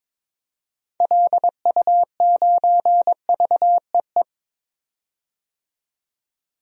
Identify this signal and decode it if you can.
Morse code, heard as a keyed tone, about 22 wpm, 700 Hz. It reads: LU9VEE
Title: DXCC-100 Call Signs - 6 Characters - 22wpm